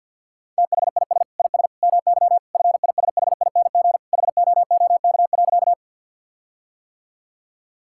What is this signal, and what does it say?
Morse code, heard as a keyed tone, about 50 wpm, 700 Hz. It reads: THIS IS MY FISHING HOOK.
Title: Sentences from Top 1600 Words 50wpm